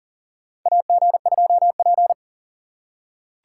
Morse code, heard as a keyed tone, about 40 wpm, 700 Hz. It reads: AG2P
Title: USA Call Signs - 4 Characters - 6x Character Spacing - 40wpm